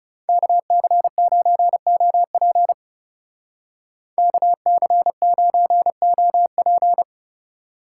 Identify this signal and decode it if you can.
Morse code, heard as a keyed tone, about 40 wpm, 700 Hz. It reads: KC9OP KC9OP